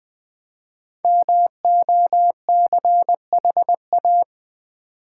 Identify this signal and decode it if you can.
Morse code, heard as a keyed tone, about 21 wpm, 690 Hz. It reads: MOCHA
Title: Easy Words with Hints - 20wpm